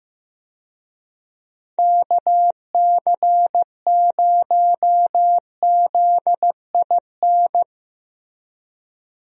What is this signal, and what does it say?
Morse code, heard as a keyed tone, about 15 wpm, 700 Hz. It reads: KC0ZIN
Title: USA Call Signs - 6 Characters - Repeated 3x - 15wpm